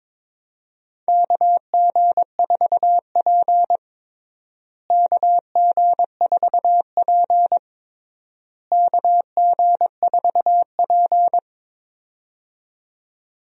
Morse code, heard as a keyed tone, about 22 wpm, 700 Hz. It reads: KG4P KG4P KG4P
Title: USA Call Signs - 4 Characters - Repeated 3x - 22wpm